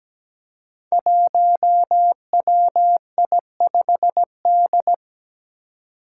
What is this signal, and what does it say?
Morse code, heard as a keyed tone, about 17 wpm, 700 Hz. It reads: 1WI5D
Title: Five Challenging Letter-Number Combinations 17wpm